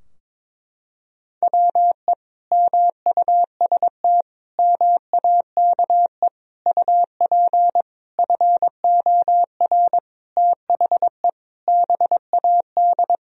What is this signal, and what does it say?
Morse code, heard as a keyed tone, about 22 wpm, 700 Hz. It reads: WE MUST MAKE UP FOR THE BAD
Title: Sentences from Top 1000 Words - Review Part 3 - 22wpm